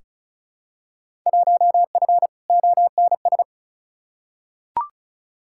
Morse code, heard as a keyed tone, about 35 wpm, 700 Hz. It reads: 1F ONS
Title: Field Day Exchanges Encore - 35wpm